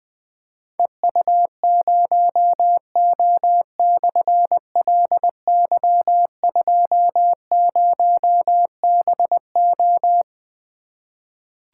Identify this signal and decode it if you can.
Morse code, heard as a keyed tone, about 20 wpm, 700 Hz. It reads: EU0O/LY20BO
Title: Difficult Call Signs 20wpm